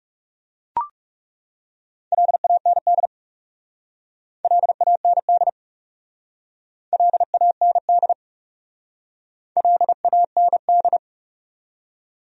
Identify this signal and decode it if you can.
Morse code, heard as a keyed tone, about 40 wpm, 700 Hz. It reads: LAND LAND LAND LAND